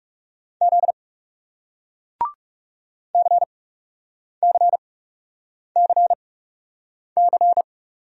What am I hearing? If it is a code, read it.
Morse code, heard as a keyed tone, about 40 wpm, spaced as for 8 wpm, 700 Hz. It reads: Z CCCC